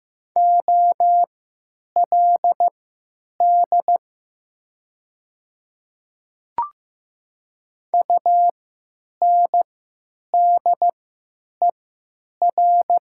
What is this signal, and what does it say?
Morse code, heard as a keyed tone, about 15 wpm, spaced as for 9 wpm, 700 Hz. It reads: OLD UNDER
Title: 3x Character Spacing - Top 300 Words - 15wpm